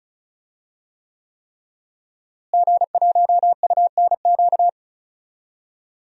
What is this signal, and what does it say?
Morse code, heard as a keyed tone, about 36 wpm, 700 Hz. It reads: G1UNQ
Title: DXCC-100 Call Signs - 5 Characters - Repeated 3x - 35wpm